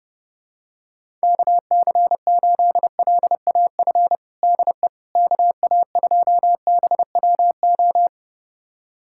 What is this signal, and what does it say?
Morse code, heard as a keyed tone, about 32 wpm, 700 Hz. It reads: KC8LAF DE KA2BWO
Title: USA Callsign DE Callsign - 6 Characters - 30wpm